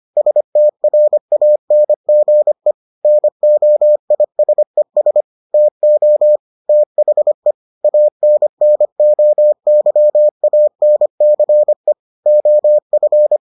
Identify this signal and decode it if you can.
Morse code, heard as a keyed tone, about 25 wpm, 600 Hz. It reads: STRANGE NOISES TO THE ANNOYANCE OF